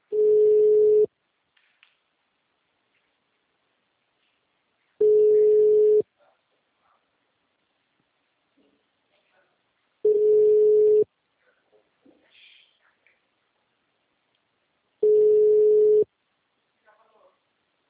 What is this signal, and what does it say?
Ringback tone (Europe, 425 Hz).